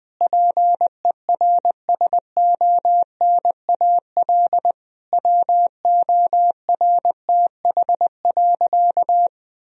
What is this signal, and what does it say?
Morse code, heard as a keyed tone, about 20 wpm, 700 Hz. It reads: PERSONAL WORTH.
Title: Sentences from Top 1000 Words - Review Part 4 - 20wpm